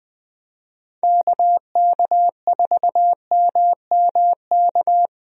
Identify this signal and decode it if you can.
Morse code, heard as a keyed tone, about 20 wpm, 700 Hz. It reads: KK4MMK
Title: USA Call Signs - 6 Characters - 20wpm